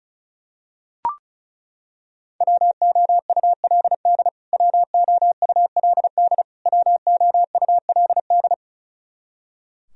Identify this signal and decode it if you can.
Morse code, heard as a keyed tone, about 34 wpm, 700 Hz. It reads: WOULD WOULD WOULD
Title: Top 100 Words Repeated 35wpm